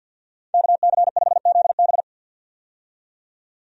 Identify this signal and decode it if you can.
Morse code, heard as a keyed tone, about 51 wpm, 700 Hz. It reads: KC5ZB
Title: USA Call Signs - 5 Characters - 50wpm